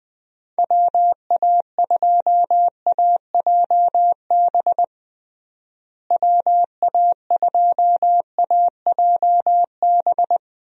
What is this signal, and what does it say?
Morse code, heard as a keyed tone, about 20 wpm, 700 Hz. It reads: WA2AJB WA2AJB
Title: USA Call Signs - 6 Characters - Repeated 3x - 20wpm